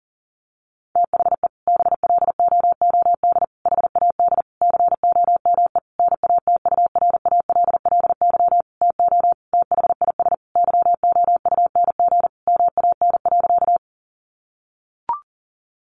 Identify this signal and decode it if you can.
Morse code, heard as a keyed tone, about 37 wpm, 700 Hz. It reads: THE BLOOD HAD COME NATURALLY TO THIS YOUNG MAN.
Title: Sentences from Top 1300 Words 40wpm